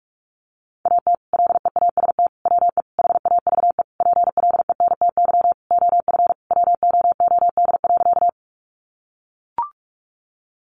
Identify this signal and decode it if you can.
Morse code, heard as a keyed tone, about 44 wpm, 700 Hz. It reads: AT LEAST WE HAVE PLENTY OF WOOD.
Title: Sentences from Top 1300 Words 45wpm